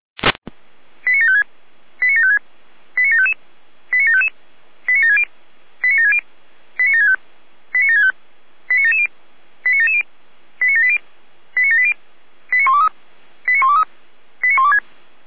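Five-tone selective call (ZVEI), heard as quick runs of five tones, tones of about 70 ms. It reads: 89756 89756 89755 89755 89760 89760 89765 89765 89770 89770 89780 89780 89123 89123 89126
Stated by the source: ZVEI five-tone call 89590 received on an SDR.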